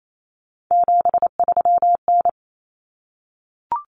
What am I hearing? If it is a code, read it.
Morse code, heard as a keyed tone, about 28 wpm, 700 Hz. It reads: 73N
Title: Common Cut Numbers - 3-Digit 28wpm